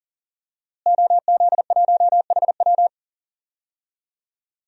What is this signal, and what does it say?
Morse code, heard as a keyed tone, about 40 wpm, 700 Hz. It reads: OZ1HW
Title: DXCC-100 Call Signs - 5 Characters - Repeated 3x - 40wpm